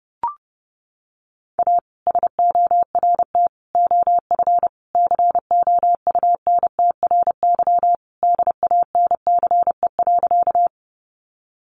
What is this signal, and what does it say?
Morse code, heard as a keyed tone, about 30 wpm, 700 Hz. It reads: A SORT OF COUNTRY DANCE.
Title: Sentences from Top 1300 Words 30wpm